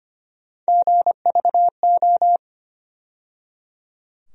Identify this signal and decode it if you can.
Morse code, heard as a keyed tone, about 25 wpm, 700 Hz. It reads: GVO
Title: USA Call Sign Postfixes - 3 Letters 25wpm